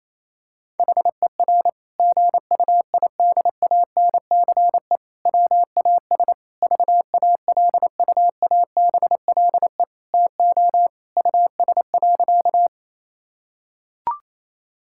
Morse code, heard as a keyed tone, about 28 wpm, 700 Hz. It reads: HER GUIDANCE WAS VALUABLE TO US.
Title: Sentences from Top 1600 Words 28wpm